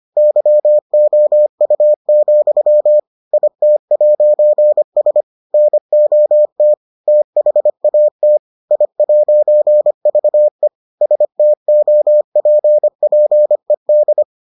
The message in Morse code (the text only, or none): YOU, IT'S NOT THAT I'VE STOPPED